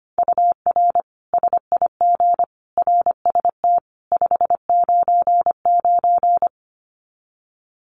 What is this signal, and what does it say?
Morse code, heard as a keyed tone, about 25 wpm, 700 Hz. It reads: UR SIG RST 599
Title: Common QSO Phrases 25wpm